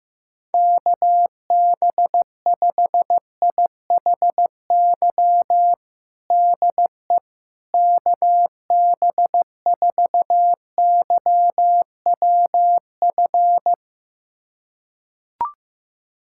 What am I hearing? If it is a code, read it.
Morse code, heard as a keyed tone, about 15 wpm, 700 Hz. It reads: KB5IHY DE KB4YWF
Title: USA Callsign DE Callsign - 6 Characters - 15wpm